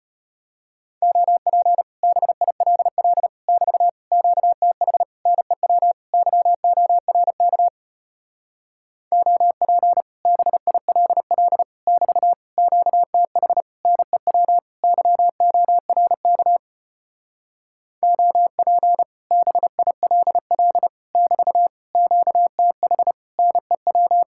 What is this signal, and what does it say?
Morse code, heard as a keyed tone, about 33 wpm, 700 Hz. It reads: OP BILL <BT> QTH NEW YORK OP BILL <BT> QTH NEW YORK OP BILL <BT> QTH NEW